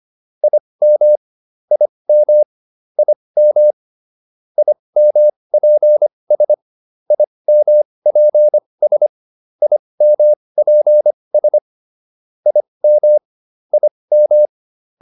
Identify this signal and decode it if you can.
Morse code, heard as a keyed tone, about 25 wpm, spaced as for 20 wpm, 600 Hz. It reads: IM IM IM IMPS IMPS IMPS IM IM